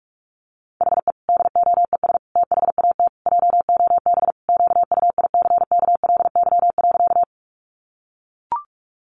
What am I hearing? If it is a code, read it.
Morse code, heard as a keyed tone, about 44 wpm, 700 Hz. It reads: HE DOES THAT JOB QUICKLY.
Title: Sentences from Top 700 Words 45wpm